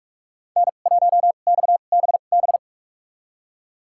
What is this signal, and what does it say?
Morse code, heard as a keyed tone, about 45 wpm, spaced as for 32 wpm, 700 Hz. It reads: N1XBB
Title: USA Call Signs - 5 Characters - 2x Character Spacing - 45wpm